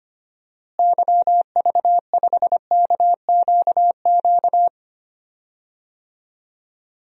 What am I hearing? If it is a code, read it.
Morse code, heard as a keyed tone, about 25 wpm, 700 Hz. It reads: YV5KQQ
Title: DXCC-100 Call Signs - 6 Characters - Repeated 3x - 25wpm